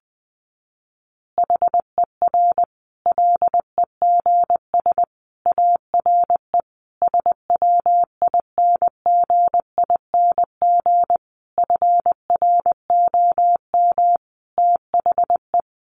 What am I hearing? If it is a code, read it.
Morse code, heard as a keyed tone, about 20 wpm, 700 Hz. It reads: HER LEGS ARE SWINGING FROM THE
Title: Easy Phrases 20wpm